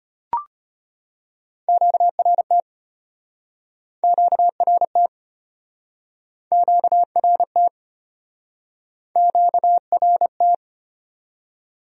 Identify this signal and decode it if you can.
Morse code, heard as a keyed tone, about 35 wpm, 700 Hz. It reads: QRT QRT QRT QRT